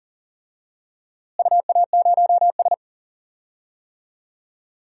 Morse code, heard as a keyed tone, about 40 wpm, 700 Hz. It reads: UA0S
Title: DXCC-100 Call Signs - 4 Characters - Repeated 3x - 40wpm